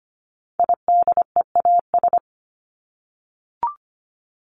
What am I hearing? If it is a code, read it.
Morse code, heard as a keyed tone, about 25 wpm, 700 Hz. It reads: IDEAS
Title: ES:S Suffix - Top 100 Words with Base Words 25wpm